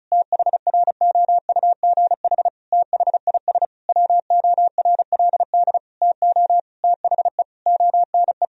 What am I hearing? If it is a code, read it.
Morse code, heard as a keyed tone, about 36 wpm, 700 Hz. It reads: THROUGH THIS WORLD TO THE ONE
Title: Sentences from Top 1000 Words - Review Part 2 - 35wpm